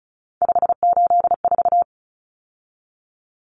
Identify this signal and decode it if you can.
Morse code, heard as a keyed tone, about 34 wpm, 700 Hz. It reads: 584
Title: Common Cut Numbers - 3-Digit 35wpm